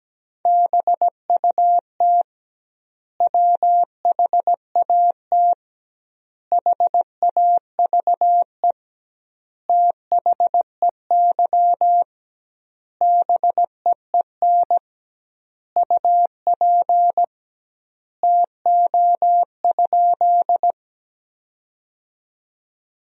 Morse code, heard as a keyed tone, about 17 wpm, 700 Hz. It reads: BUT WHAT HAVE THEY BEEN UP TO?
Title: 2x Word-Spacing - Sentences from Top 100 Words - 17wpm